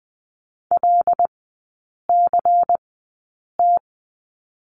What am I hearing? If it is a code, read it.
Morse code, heard as a keyed tone, about 20 wpm, spaced as for 8 wpm, 700 Hz. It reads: LCT
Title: 114 Lesson 19e - Letter w - Three Characters 20wpm